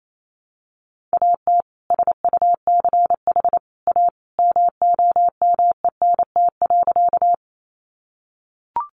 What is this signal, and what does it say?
Morse code, heard as a keyed tone, about 28 wpm, 700 Hz. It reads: AT SUCH A MOMENT.
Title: Sentences from Top 1000 Words 28wpm